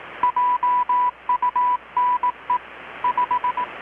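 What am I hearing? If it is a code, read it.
Morse code, heard as a keyed tone, about 18 wpm, 990 Hz, with radio noise. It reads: JUNE 5
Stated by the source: Morse from KPH's Night of Nights XX maritime radiotelegraphy event, received in Chiapas, Mexico about 3,800 km away on a Kenwood R-600 with a grounded 10 m long-wire antenna.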